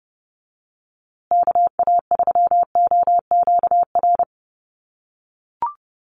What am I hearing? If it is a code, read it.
Morse code, heard as a keyed tone, about 28 wpm, 700 Hz. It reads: KA3OQR
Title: USA Call Signs Repeated 3x 30wpm